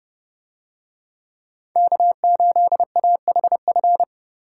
Morse code, heard as a keyed tone, about 29 wpm, 700 Hz. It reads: K8AHF